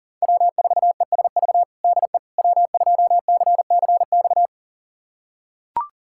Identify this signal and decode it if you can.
Morse code, heard as a keyed tone, about 40 wpm, 700 Hz. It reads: W4ESV DE W2CCX
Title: USA Callsign DE Callsign - 5 Characters - 40wpm